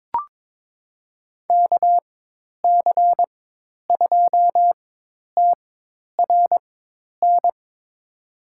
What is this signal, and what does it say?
Morse code, heard as a keyed tone, about 22 wpm, spaced as for 10 wpm, 700 Hz. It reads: KC2TRN
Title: USA Call Signs - 6 Characters - 4x Character Spacing - 22wpm